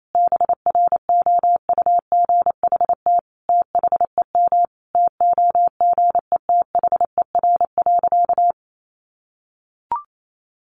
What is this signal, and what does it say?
Morse code, heard as a keyed tone, about 28 wpm, 700 Hz. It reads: BROUGHT THEM TOGETHER.
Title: Sentences from Top 1000 Words 28wpm